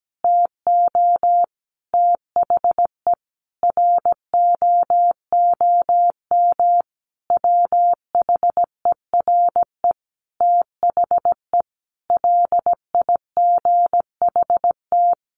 Morse code, heard as a keyed tone, about 17 wpm, 700 Hz. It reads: TO THE ROOM WHERE THE LIGHT
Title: Sentences from Top 1000 Words 17wpm